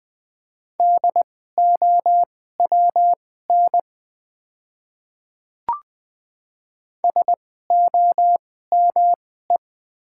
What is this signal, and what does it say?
Morse code, heard as a keyed tone, about 20 wpm, spaced as for 14 wpm, 700 Hz. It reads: DOWN SOME
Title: 2x Character Spacing - Top 100 Words - 20wpm